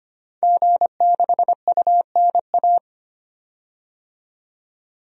Morse code, heard as a keyed tone, about 25 wpm, 700 Hz. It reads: G6UNA
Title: DXCC-100 Call Signs - 5 Characters - 25wpm